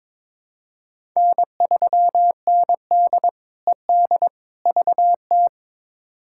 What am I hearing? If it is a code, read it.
Morse code, heard as a keyed tone, about 22 wpm, 700 Hz. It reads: N3ND ED VT
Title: K1USN SST QSO Practice - 22wpm